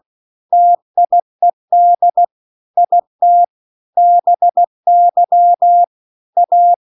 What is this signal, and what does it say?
Morse code, heard as a keyed tone, about 16 wpm, 700 Hz. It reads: TIED IT BY A
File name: CW-The-Big-Twoheaded-River-2-16WPM